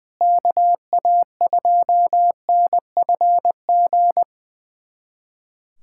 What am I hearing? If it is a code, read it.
Morse code, heard as a keyed tone, about 20 wpm, 700 Hz. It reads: KA2NFG
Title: USA Callsign DE Callsign - 6 Characters - 20wpm